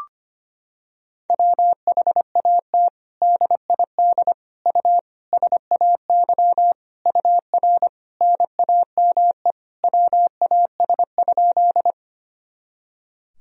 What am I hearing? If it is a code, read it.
Morse code, heard as a keyed tone, about 25 wpm, 700 Hz. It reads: WHAT DID U SAY UR NAME WAS?
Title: Sentences from Top 200 Words and Abbreviations 25wpm